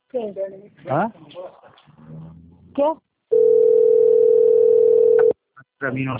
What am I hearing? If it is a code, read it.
Ringback tone (North America), 440 + 480 Hz, with speech.
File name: out- - - - - .37802